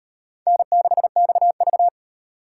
Morse code, heard as a keyed tone, about 36 wpm, 700 Hz. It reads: N6XV